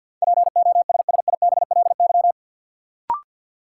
Morse code, heard as a keyed tone, about 53 wpm, 700 Hz. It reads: POSSIBLY
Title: Sentences from Top 1300 Words - New Word Drill - 50wpm